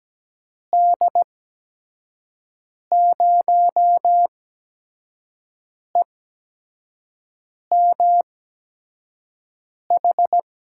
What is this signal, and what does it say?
Morse code, heard as a keyed tone, about 17 wpm, spaced as for 5 wpm, 700 Hz. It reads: D0EMH
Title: USA Call Signs - 6 Characters - 8x Character Spacing - 17wpm